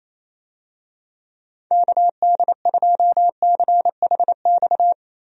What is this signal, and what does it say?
Morse code, heard as a keyed tone, about 28 wpm, 700 Hz. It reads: KD2CHX
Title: USA Call Signs - 6 Characters - Repeated 2x - 28wpm